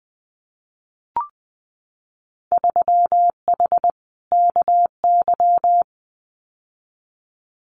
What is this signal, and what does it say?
Morse code, heard as a keyed tone, about 20 wpm, 700 Hz. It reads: 3H KY
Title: Winter Field Day Exchanges Encore 20wpm